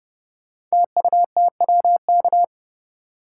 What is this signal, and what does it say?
Morse code, heard as a keyed tone, about 30 wpm, 700 Hz. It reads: TUTWK
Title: Code Groups - 5 Letters - 30wpm